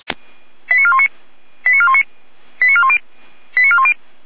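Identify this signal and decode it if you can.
Five-tone selective call (ZVEI), heard as quick runs of five tones, tones of about 70 ms. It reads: 89419 89419 89410 89410
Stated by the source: ZVEI five-tone call received on an SDR.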